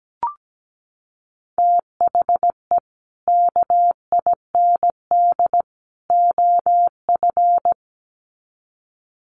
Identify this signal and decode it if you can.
Morse code, heard as a keyed tone, about 17 wpm, 700 Hz. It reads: THE KIND OF
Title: Sets of 3 Words 17wpm